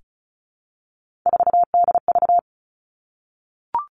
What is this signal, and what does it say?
Morse code, heard as a keyed tone, about 36 wpm, 700 Hz. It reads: 4DV
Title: Three Challenging Letter-Number Combinations 35wpm